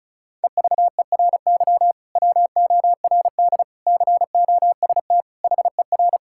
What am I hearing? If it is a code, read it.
Morse code, heard as a keyed tone, about 36 wpm, 700 Hz. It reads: EVERY WORD COST HER
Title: Sentences from Top 1600 Words 35wpm